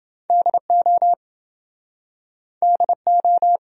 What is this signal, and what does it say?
Morse code, heard as a keyed tone, about 29 wpm, 700 Hz. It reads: DO DO